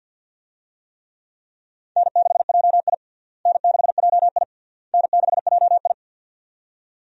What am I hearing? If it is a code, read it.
Morse code, heard as a keyed tone, about 50 wpm, 700 Hz. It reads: N6JI N6JI N6JI
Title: USA Call Signs - 4 Characters - Repeated 3x - 50wpm